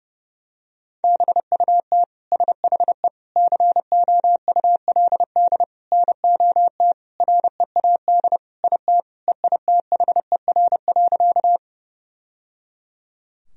Morse code, heard as a keyed tone, about 32 wpm, 700 Hz. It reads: BUT SHE COULD NOT READ IT EITHER.
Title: Sentences from Top 1000 Words - Review Part 4 - 30wpm